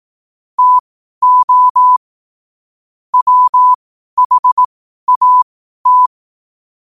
Morse code, heard as a keyed tone, about 18 wpm, spaced as for 12 wpm, 1000 Hz. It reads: TO WHAT
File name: qotdfiles / 12wpm